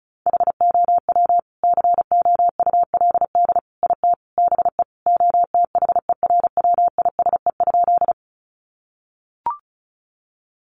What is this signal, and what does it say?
Morse code, heard as a keyed tone, about 34 wpm, 700 Hz. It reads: HOW COULD IT BE OTHERWISE?
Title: Sentences from Top 1200 Words 35wpm